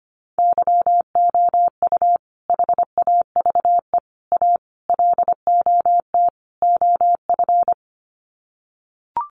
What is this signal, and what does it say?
Morse code, heard as a keyed tone, about 25 wpm, 700 Hz. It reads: YOU HAVE A LOT OF
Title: Sets of 5 Words Encore 25wpm